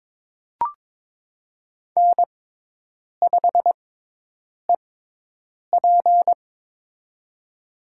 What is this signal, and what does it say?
Morse code, heard as a keyed tone, about 21 wpm, spaced as for 8 wpm, 700 Hz. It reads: N5EP
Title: USA Call Signs - 4 Characters - 6x Character Spacing - 22wpm